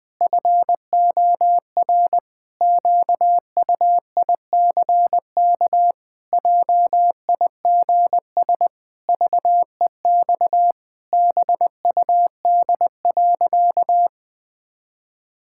Morse code, heard as a keyed tone, about 19 wpm, 700 Hz. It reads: FOR QUICK JIGS VEX BUD.
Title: Pangram Challenge v2 - 20wpm